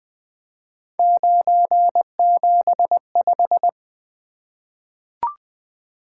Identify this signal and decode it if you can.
Morse code, heard as a keyed tone, about 20 wpm, 700 Hz. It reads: 975
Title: Three Letter-Number Combinations - 20wpm